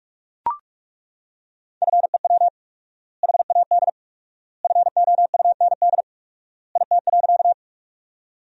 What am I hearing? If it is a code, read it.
Morse code, heard as a keyed tone, about 41 wpm, 700 Hz. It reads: FEW HAD FOUND IT.
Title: 4x Word-Spacing - Sentences from Top 300 Words - 45wpm